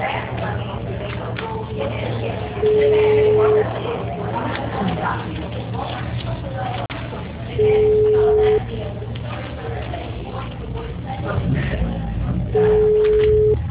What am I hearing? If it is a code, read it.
Ringback tone (Europe, 425 Hz), with speech.
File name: out- -2007- - - .37799